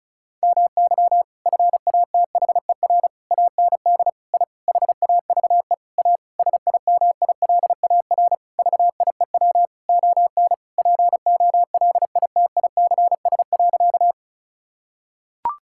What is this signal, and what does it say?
Morse code, heard as a keyed tone, about 34 wpm, 700 Hz. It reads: MY FATHER AND I HAVE A SIMILAR VIEW ON POLITICS.
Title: Sentences from Top 1000 Words - Review Part 6 - 35wpm